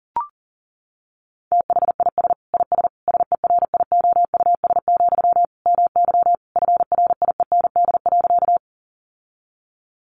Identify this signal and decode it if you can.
Morse code, heard as a keyed tone, about 37 wpm, 700 Hz. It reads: THIS IS SERIOUS, MY FRIEND.
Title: Sentences from Top 1100 Words 40wpm